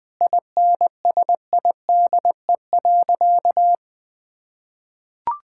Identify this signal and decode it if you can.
Morse code, heard as a keyed tone, about 20 wpm, 700 Hz. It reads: INSIDE.
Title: Sentences with 2-Word Phrases - 20wpm